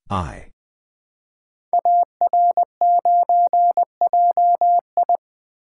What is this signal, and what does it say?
Morse code, heard as a keyed tone, about 20 wpm, 700 Hz. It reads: AR9JI
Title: Code Groups - 5 Letter:Numbers - 20wpm